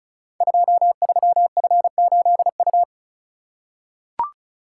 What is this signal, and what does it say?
Morse code, heard as a keyed tone, about 36 wpm, 700 Hz. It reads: 23F8U
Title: Code Groups - 5 Letter:Numbers - 35wpm